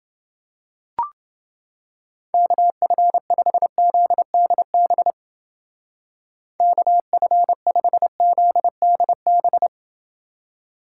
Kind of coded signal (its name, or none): Morse code